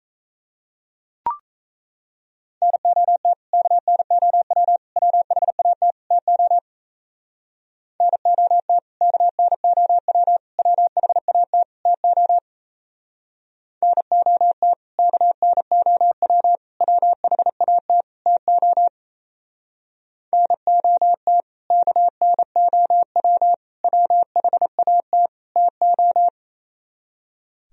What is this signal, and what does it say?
Morse code, heard as a keyed tone, about 35 wpm, 700 Hz. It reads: NOT KNOW WHAT TO NOT KNOW WHAT TO NOT KNOW WHAT TO NOT KNOW WHAT TO